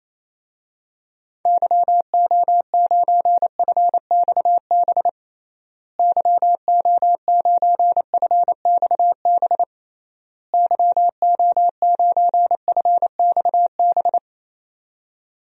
Morse code, heard as a keyed tone, about 28 wpm, 700 Hz. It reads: YO9FXB YO9FXB YO9FXB
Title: DXCC-100 Call Signs - 6 Characters - Repeated 3x - 28wpm